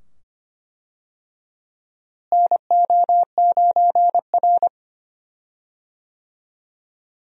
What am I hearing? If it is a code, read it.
Morse code, heard as a keyed tone, about 25 wpm, 700 Hz. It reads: NO9R
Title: USA Call Signs - 4 Characters - Repeated 2x - 25wpm